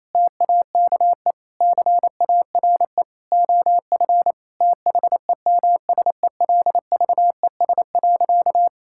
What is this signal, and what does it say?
Morse code, heard as a keyed tone, about 28 wpm, 700 Hz. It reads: TAKE CARE OF THEMSELVES.
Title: Sentences from Top 1000 Words - Review Part 5 - 28wpm